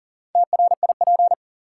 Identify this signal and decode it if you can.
Morse code, heard as a keyed tone, about 40 wpm, 700 Hz. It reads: TRIP
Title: Sentences from Top 1000 Words - Review Part 4 - New Word Drill - 40wpm